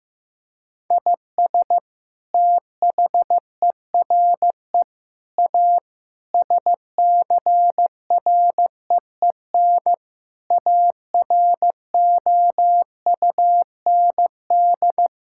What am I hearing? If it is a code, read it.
Morse code, heard as a keyed tone, about 15 wpm, 700 Hz. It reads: IS THERE A SCREEN AROUND
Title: Sentences from Top 1000 Words - Review Part 6 - 15wpm